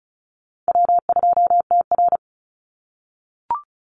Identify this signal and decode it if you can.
Morse code, heard as a keyed tone, about 34 wpm, 700 Hz. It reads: W2TR
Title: Call Signs - 35wpm